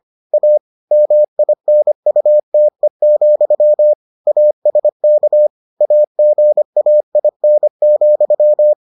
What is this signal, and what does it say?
Morse code, heard as a keyed tone, about 25 wpm, 600 Hz. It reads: A MINUTE, ASK AGAIN,